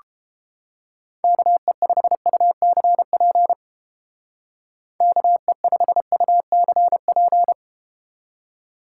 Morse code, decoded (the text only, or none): KE5UCP KE5UCP